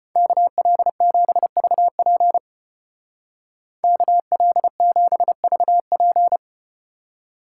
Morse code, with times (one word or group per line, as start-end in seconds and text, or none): 0.2-2.4 KL7VP
3.8-6.4 KL7VP